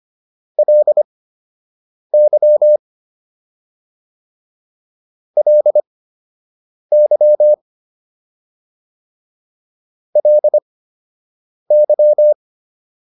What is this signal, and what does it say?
Morse code, heard as a keyed tone, about 25 wpm, spaced as for 7 wpm, 600 Hz. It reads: LY LY LY
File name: LY6 07